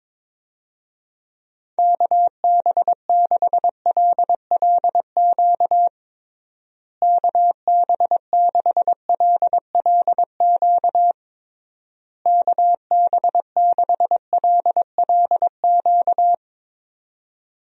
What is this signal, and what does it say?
Morse code, heard as a keyed tone, about 21 wpm, 700 Hz. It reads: KB6LLQ KB6LLQ KB6LLQ
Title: USA Call Signs - 6 Characters - Repeated 3x - 22wpm